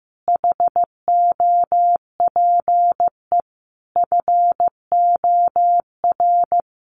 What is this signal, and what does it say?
Morse code, heard as a keyed tone, about 15 wpm, 700 Hz. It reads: HOPE FOR
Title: Sentences from Top 1000 Words 15wpm